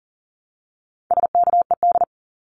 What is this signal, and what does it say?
Morse code, heard as a keyed tone, about 40 wpm, 700 Hz. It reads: SKED
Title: Abbreviations Repeated 3x - 40wpm